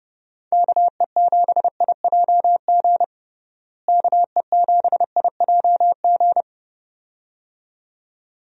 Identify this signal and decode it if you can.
Morse code, heard as a keyed tone, about 32 wpm, 700 Hz. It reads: KE7IJG KE7IJG
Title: USA Call Signs - 6 Characters - Repeated 2x - 30wpm